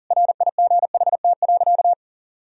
Morse code, heard as a keyed tone, about 40 wpm, 700 Hz. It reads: RIGHT.
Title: Sentences from Top 1000 Words - Review Part 4 - 40wpm